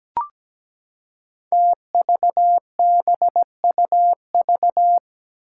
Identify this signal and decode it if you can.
Morse code, heard as a keyed tone, about 17 wpm, 700 Hz. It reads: TVBUV
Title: Five Challenging Letter-Number Combinations 17wpm